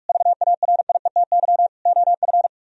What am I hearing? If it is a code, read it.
Morse code, heard as a keyed tone, about 49 wpm, 690 Hz. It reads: VARIETY OF